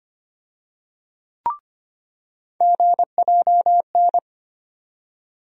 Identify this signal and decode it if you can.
Morse code, heard as a keyed tone, about 25 wpm, 700 Hz. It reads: GJN
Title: USA Call Sign Postfixes - 3 Letters 25wpm